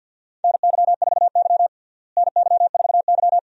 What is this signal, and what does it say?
Morse code, heard as a keyed tone, about 49 wpm, 700 Hz. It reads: NY4Y NY4Y
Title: USA Call Signs - 4 Characters - Repeated 2x - 50wpm